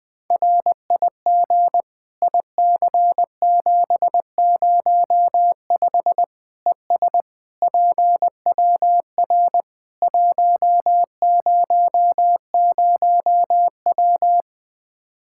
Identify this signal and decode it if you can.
Morse code, heard as a keyed tone, about 20 wpm, 700 Hz. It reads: RIG IC705 ES PWR 100W
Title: Practice QSO - Exchange 2 - Info - 20wpm